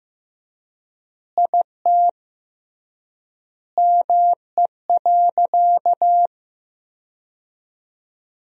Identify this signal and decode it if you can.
Morse code, heard as a keyed tone, about 15 wpm, 700 Hz. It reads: IT ME.
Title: 3x Word-Spacing - Sentences from Top 200 Words - 15wpm